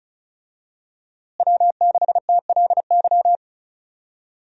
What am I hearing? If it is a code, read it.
Morse code, heard as a keyed tone, about 36 wpm, 700 Hz. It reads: W6TLY
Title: USA Call Signs - 5 Characters - Repeated 3x - 35wpm